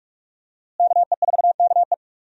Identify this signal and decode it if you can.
Morse code, heard as a keyed tone, about 46 wpm, 700 Hz. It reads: KE4KE
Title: USA Call Signs - 5 Characters - Repeated 2x - 45wpm